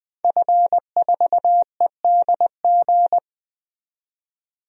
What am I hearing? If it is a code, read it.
Morse code, heard as a keyed tone, about 20 wpm, 700 Hz. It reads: F4EDG
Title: DXCC-100 Call Signs - 5 Characters - 20wpm